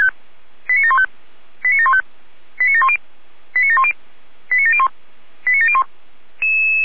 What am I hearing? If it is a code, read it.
Five-tone selective call (ZVEI), heard as quick runs of five tones, tones of about 70 ms. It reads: 89715 89715 89710 89710 89701 89701